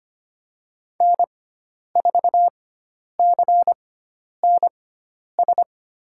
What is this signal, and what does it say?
Morse code, heard as a keyed tone, about 25 wpm, spaced as for 10 wpm, 700 Hz. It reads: N4CNS
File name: Lesson 16g - Letter c - Call Signs 25wpm